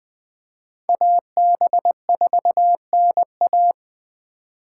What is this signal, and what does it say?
Morse code, heard as a keyed tone, about 20 wpm, 700 Hz. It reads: AB4NA
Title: Speed Racing - USA 2x2 Call Signs 17wpm